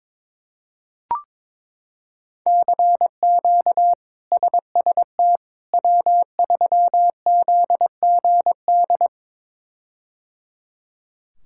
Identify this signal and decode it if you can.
Morse code, heard as a keyed tone, about 22 wpm, 700 Hz. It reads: CQ SST W3ZGD
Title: K1USN SST QSO Practice - 22wpm